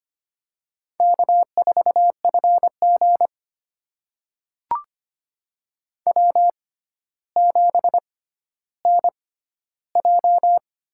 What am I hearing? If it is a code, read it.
Morse code, heard as a keyed tone, about 25 wpm, spaced as for 13 wpm, 700 Hz. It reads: K4FG W7NJ
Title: USA Call Signs - 4 Characters - 6x Character Spacing - 25wpm